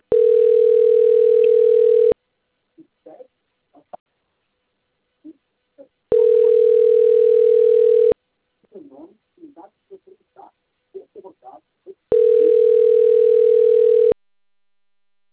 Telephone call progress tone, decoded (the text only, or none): ringback tone